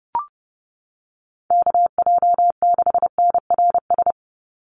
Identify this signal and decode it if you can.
Morse code, heard as a keyed tone, about 30 wpm, 700 Hz. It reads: KJ6NRS
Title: USA Call Signs 30wpm